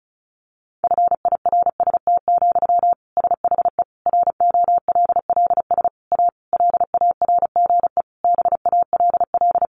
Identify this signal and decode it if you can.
Morse code, heard as a keyed tone, about 34 wpm, 700 Hz. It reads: FIRST, SHE ROLLS A LARGE BALL
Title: Easy Phrases 35wpm